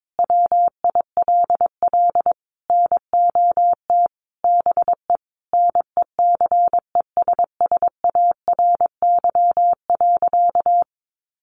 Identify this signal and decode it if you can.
Morse code, heard as a keyed tone, about 22 wpm, 700 Hz. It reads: WILL NOT BE NECESSARY.
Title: Sentences from Top 800 Words 22wpm